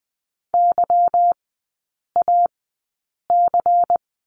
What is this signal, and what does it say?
Morse code, heard as a keyed tone, about 20 wpm, spaced as for 8 wpm, 690 Hz. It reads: YAC
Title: 166 Lesson 27e - Number 7 - Three Characters 20wpm